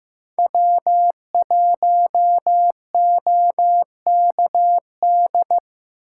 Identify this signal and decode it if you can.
Morse code, heard as a keyed tone, about 15 wpm, 700 Hz. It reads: W1OKD
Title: USA Call Signs - 5 Characters - 6x Character Spacing - 15wpm